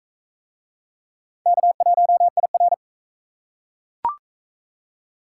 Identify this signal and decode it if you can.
Morse code, heard as a keyed tone, about 38 wpm, 700 Hz. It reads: K1IR